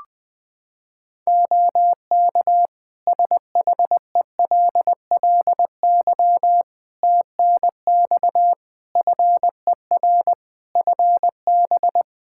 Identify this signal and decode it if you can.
Morse code, heard as a keyed tone, about 21 wpm, 700 Hz. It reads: OK SHELLY TNX FER FB
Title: Practice QSO - Exchange 4 - Ending - 20wpm